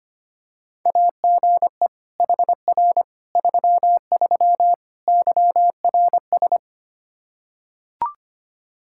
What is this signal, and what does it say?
Morse code, heard as a keyed tone, about 25 wpm, 700 Hz. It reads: AGE HR 33 YRS
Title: K1USN SST QSO Practice 25wpm